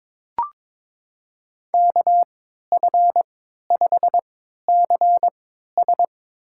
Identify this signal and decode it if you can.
Morse code, heard as a keyed tone, about 22 wpm, spaced as for 12 wpm, 700 Hz. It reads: KF5CS
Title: USA Call Signs - 3x Character Spacing - 22wpm